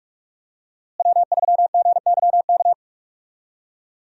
Morse code, heard as a keyed tone, about 41 wpm, 700 Hz. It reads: W3GYK
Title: USA Call Signs - 5 Characters - Repeated 3x - 45wpm